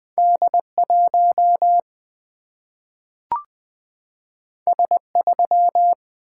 Morse code, heard as a keyed tone, about 20 wpm, 700 Hz. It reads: D1 S3
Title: Single-Letter Single-Number 20wpm